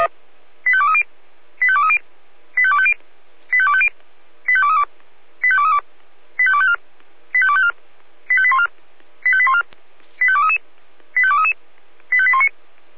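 Five-tone selective call (ZVEI), heard as quick runs of five tones, tones of about 80 ms. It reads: 85230 85230 85250 85250 85232 85232 85254 85254 85814 85814 85233 85233 85819
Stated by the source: ZVEI five-tone selective call received on SDR.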